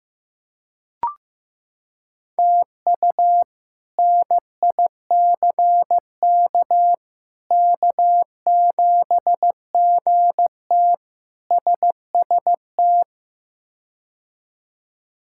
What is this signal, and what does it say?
Morse code, heard as a keyed tone, about 15 wpm, 700 Hz. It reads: TU NICK K7GT SST
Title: K1USN SST QSO Practice - 15wpm